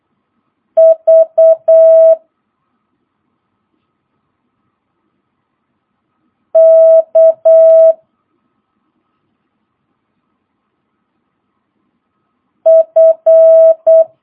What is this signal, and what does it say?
Morse code, heard as a keyed tone, about 8 wpm, 650 Hz. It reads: V K F
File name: jh8cbh / 2022cw